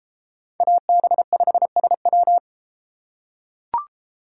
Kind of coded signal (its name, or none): Morse code